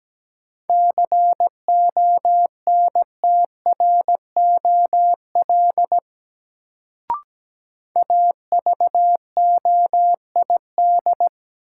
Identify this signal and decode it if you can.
Morse code, heard as a keyed tone, about 17 wpm, 700 Hz. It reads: CONTROL AVOID
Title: Sentences from Top 1200 Words - New Word Drill Alt - 17wpm